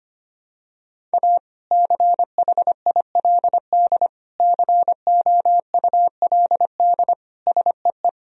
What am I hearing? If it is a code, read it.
Morse code, heard as a keyed tone, about 25 wpm, 700 Hz. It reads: A CHILD COULD SEE
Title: Sentences from Top 1000 Words - Review Part 1 - 25wpm